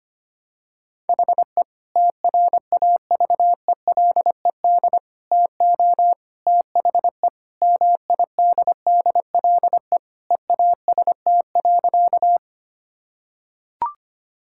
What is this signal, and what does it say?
Morse code, heard as a keyed tone, about 25 wpm, 700 Hz. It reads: HE TRAVELED TO THE MIDDLE EAST.
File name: practice / 25wpm / Sentences with 3-Word Phrases - 25wpm